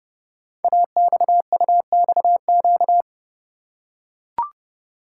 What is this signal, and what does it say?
Morse code, heard as a keyed tone, about 30 wpm, 700 Hz. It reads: AXUXQ
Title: Code Groups - 5 Letters - 30wpm